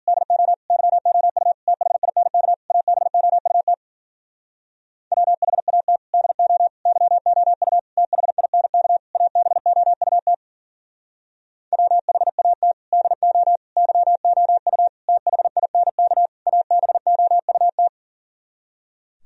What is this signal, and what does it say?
Morse code, heard as a keyed tone, about 54 wpm, 700 Hz. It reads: DO YOU THINK ABOUT WHAT DO YOU THINK ABOUT WHAT DO YOU THINK ABOUT